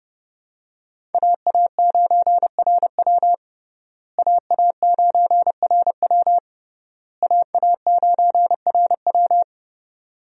Morse code, heard as a keyed tone, about 30 wpm, 700 Hz. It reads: AA9RW AA9RW AA9RW
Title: USA Call Signs - 5 Characters - Repeated 3x - 30wpm